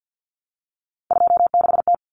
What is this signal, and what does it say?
Morse code, heard as a keyed tone, about 50 wpm, 700 Hz. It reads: 26T
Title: Common Cut Numbers - 3-Digit 50wpm